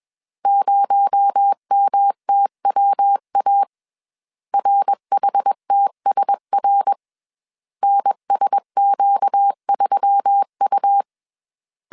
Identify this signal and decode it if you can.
Morse code, heard as a keyed tone, about 21 wpm, 800 Hz. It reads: 0MTWA RHTSR NSQ3U
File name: CW / DET / 1 6 1500C S C 20wpm 2116